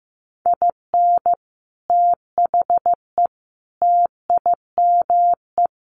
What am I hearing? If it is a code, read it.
Morse code, heard as a keyed tone, about 15 wpm, 700 Hz. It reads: IN THE TIME
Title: A Sound Of Thunder 15wpm